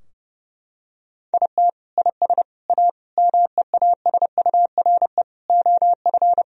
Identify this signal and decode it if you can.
Morse code, heard as a keyed tone, about 30 wpm, 700 Hz. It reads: IT IS A MEASURE OF
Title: Sentences from Top 1000 Words - Review Part 2 - 30wpm